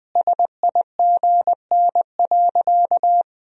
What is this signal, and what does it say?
Morse code, heard as a keyed tone, about 20 wpm, 700 Hz. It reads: SIGN.
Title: Sentences from Top 1000 Words - Review Part 4 - 20wpm